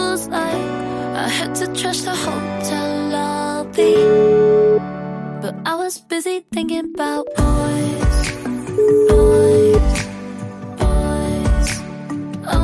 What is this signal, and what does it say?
Ringback tone (Europe, 425 Hz), with noise.